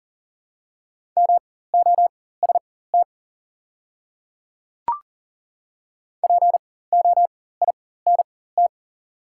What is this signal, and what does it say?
Morse code, heard as a keyed tone, about 40 wpm, spaced as for 19 wpm, 690 Hz. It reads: MOST POINT
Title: 4x Character Spacing - Top 200 Words - 40wpm